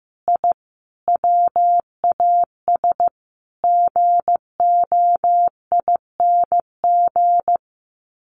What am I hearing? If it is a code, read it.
Morse code, heard as a keyed tone, about 15 wpm, 700 Hz. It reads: I WAS GOING
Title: Sets of 5 Words Encore2 15wpm